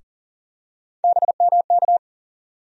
Morse code, heard as a keyed tone, about 40 wpm, 700 Hz. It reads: BMK
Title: USA Call Sign Postfixes - 3 Letters 40wpm